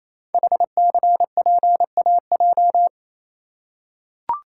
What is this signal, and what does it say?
Morse code, heard as a keyed tone, about 28 wpm, 700 Hz. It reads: HCPAJ
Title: Code Groups - 5 Letters - 28wpm